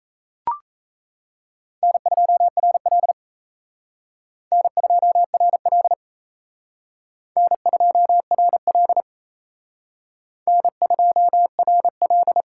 Morse code, heard as a keyed tone, about 35 wpm, 700 Hz. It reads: N2RL N2RL N2RL N2RL